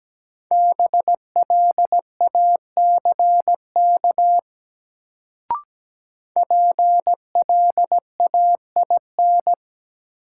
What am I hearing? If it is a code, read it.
Morse code, heard as a keyed tone, about 17 wpm, 700 Hz. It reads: BLACK PLAIN
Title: Sentences from Top 400 Words - New Word Drill - 17wpm